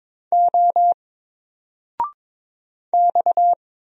Morse code, heard as a keyed tone, about 22 wpm, 700 Hz. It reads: O X
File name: Single-Letter-Number-22wpm